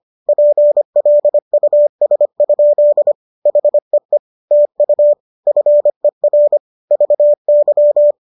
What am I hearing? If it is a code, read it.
Morse code, heard as a keyed tone, about 25 wpm, 590 Hz. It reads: PLUS? HEE TU FER VY